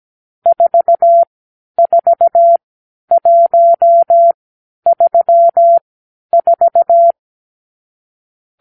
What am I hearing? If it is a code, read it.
Morse code, heard as a keyed tone, about 17 wpm, spaced as for 10 wpm, 680 Hz. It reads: 44134
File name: k7qo / Lesson 46 (1-4)